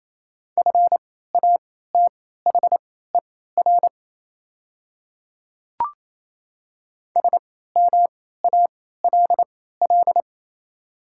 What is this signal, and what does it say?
Morse code, heard as a keyed tone, about 28 wpm, spaced as for 16 wpm, 690 Hz. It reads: FATHER SMALL
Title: 3x Character Spacing - Top 500 Words - 28wpm